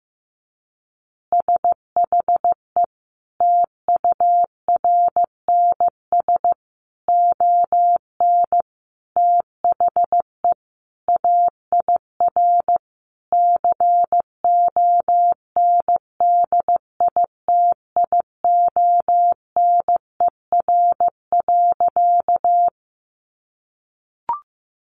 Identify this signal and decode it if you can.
Morse code, heard as a keyed tone, about 15 wpm, 700 Hz. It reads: SHE TURNS ON THE AIR CONDITIONER.
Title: Additional Easy Phrases 15wpm